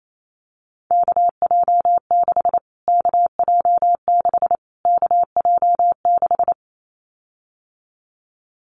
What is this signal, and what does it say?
Morse code, heard as a keyed tone, about 28 wpm, 700 Hz. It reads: KJ6 KJ6 KJ6
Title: Callsign Prefixes for Top 25 DXCC - Repeated 3x - 28wpm